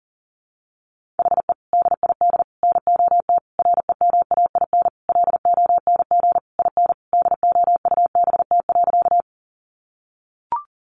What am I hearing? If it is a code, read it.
Morse code, heard as a keyed tone, about 40 wpm, 700 Hz. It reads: HE DID NOT REMAIN LONG IN DOUBT.
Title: Sentences from Top 1500 Words 40wpm